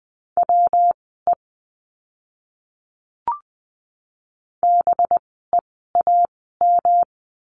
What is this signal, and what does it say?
Morse code, heard as a keyed tone, about 20 wpm, spaced as for 14 wpm, 700 Hz. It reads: WE BEAM
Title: 185 Lesson 30f - Letter b - Words 20wpm